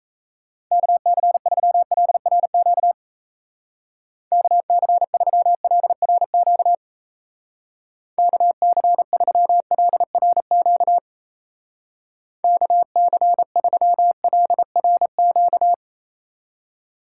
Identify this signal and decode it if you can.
Morse code, heard as a keyed tone, about 35 wpm, 700 Hz. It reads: KC3LRQ KC3LRQ KC3LRQ KC3LRQ
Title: Speed Racing - USA 2x3 Call Signs 28wpm